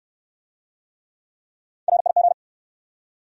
Morse code, heard as a keyed tone, about 68 wpm, 700 Hz. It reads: RIG